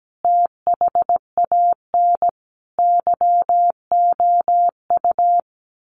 Morse code, heard as a keyed tone, about 17 wpm, 700 Hz. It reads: THAN YOU
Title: Sentences from Top 1400 Words 17wpm